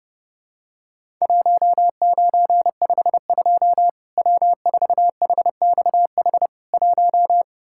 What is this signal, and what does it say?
Morse code, heard as a keyed tone, about 30 wpm, 700 Hz. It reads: 1952 W4HXH 1
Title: Worked All Europe DX Contest - QTC Exchange - 30wpm